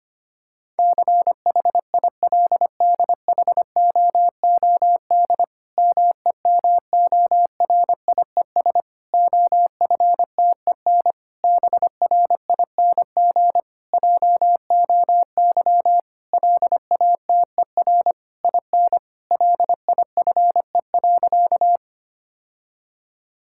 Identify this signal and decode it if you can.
Morse code, heard as a keyed tone, about 25 wpm, 700 Hz. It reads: CHILDHOOD MEMORIES OFTEN BRING JOY LATER IN LIFE.
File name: practice / 25wpm / Difficult Sentences with Hints - 25wpm